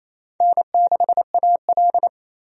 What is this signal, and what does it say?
Morse code, heard as a keyed tone, about 28 wpm, 700 Hz. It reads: N6AL
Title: USA Call Signs - 4 Characters - 28wpm